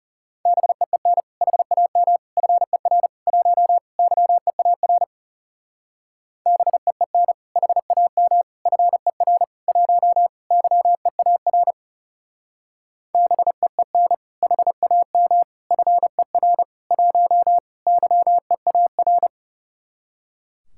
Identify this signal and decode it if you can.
Morse code, heard as a keyed tone, about 37 wpm, 700 Hz. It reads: BEEN HAM FER 1 YEAR BEEN HAM FER 1 YEAR BEEN HAM FER 1 YEAR
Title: Speed-Racing Common QSO Phrases Encore - 30wpm